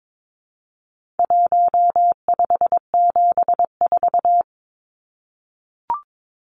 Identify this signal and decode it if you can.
Morse code, heard as a keyed tone, about 22 wpm, 700 Hz. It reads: 1574
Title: Numbers - 4-Digit 22wpm